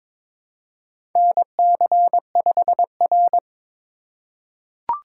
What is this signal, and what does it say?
Morse code, heard as a keyed tone, about 22 wpm, 700 Hz. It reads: NC5R
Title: USA Call Signs - 4 Characters - 3x Character Spacing - 22wpm